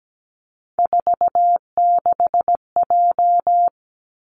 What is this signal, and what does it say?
Morse code, heard as a keyed tone, about 17 wpm, 700 Hz. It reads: 46J
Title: Three Challenging Letter-Number Combinations 17wpm